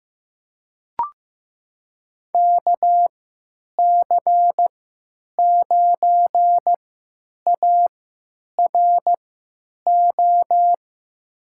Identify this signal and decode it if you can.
Morse code, heard as a keyed tone, about 15 wpm, spaced as for 9 wpm, 700 Hz. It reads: KC9ARO
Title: USA Call Signs - 3x Character Spacing - 15wpm